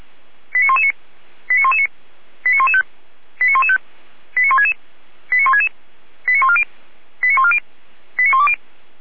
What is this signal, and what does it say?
Five-tone selective call (ZVEI), heard as quick runs of five tones, tones of about 70 ms. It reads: 89119 89119 89115 89115 89166 89166 89140 89140 89120